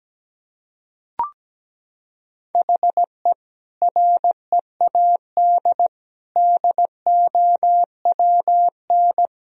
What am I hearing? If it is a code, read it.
Morse code, heard as a keyed tone, about 17 wpm, 700 Hz. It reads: HE READ DOWN
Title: Sentences from Top 1000 Words - Review Part 1 - 17wpm